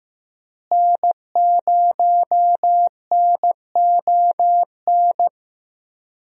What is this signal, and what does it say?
Morse code, heard as a keyed tone, about 15 wpm, 700 Hz. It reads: N0NON
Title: USA Call Signs - 5 Characters - Repeated 3x - 15wpm